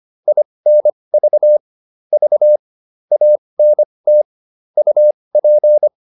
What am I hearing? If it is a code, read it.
Morse code, heard as a keyed tone, about 25 wpm, spaced as for 20 wpm, 600 Hz. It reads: INV V ANT UP